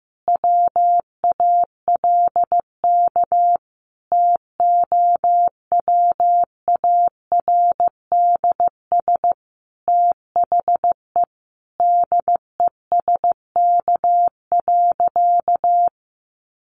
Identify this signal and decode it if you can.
Morse code, heard as a keyed tone, about 15 wpm, 700 Hz. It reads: WALK TOWARDS THE DESK.
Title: Additional Easy Phrases 15wpm